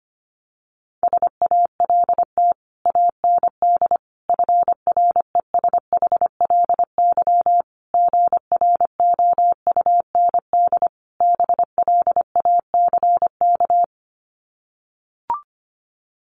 Morse code, heard as a keyed tone, about 25 wpm, 700 Hz. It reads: SALT AND FRESHLY GROUND BLACK
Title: Sets of 5 Words Encore 25wpm